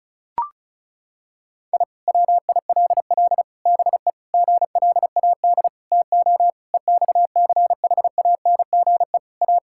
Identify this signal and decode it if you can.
Morse code, heard as a keyed tone, about 36 wpm, 700 Hz. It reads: I WILL BE GLAD TO EXCHANGE A
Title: Sentences from Top 1000 Words - Review Part 6 - 35wpm